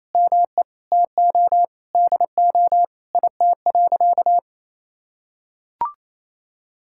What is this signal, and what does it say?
Morse code, heard as a keyed tone, about 27 wpm, 700 Hz. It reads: ME TO DO IT.
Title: Sentences from Top 1000 Words - Review Part 5 - 28wpm